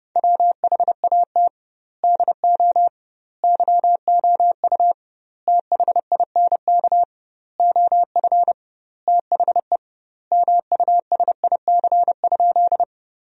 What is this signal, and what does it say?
Morse code, heard as a keyed tone, about 30 wpm, 700 Hz. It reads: WHAT DO YOU THINK OF THE MUSIC?
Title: 2x Word-Spacing - Sentences from Top 300 Words - 30wpm